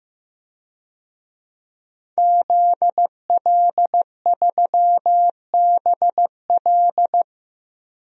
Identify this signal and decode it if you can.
Morse code, heard as a keyed tone, about 15 wpm, 700 Hz. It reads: ZL3BL
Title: DXCC-100 Call Signs - 5 Characters - 15wpm